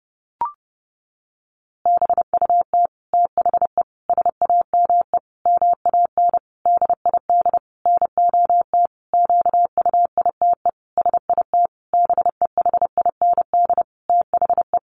Morse code, heard as a keyed tone, about 28 wpm, 700 Hz. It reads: BUT THE SAME MAN DID NOT QUITE SIT BEHIND THE
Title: A Sound Of Thunder 30wpm